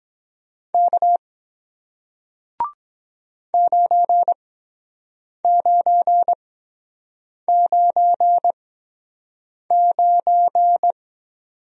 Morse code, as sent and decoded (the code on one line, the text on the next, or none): -.- / ----. ----. ----. ----.
K 9999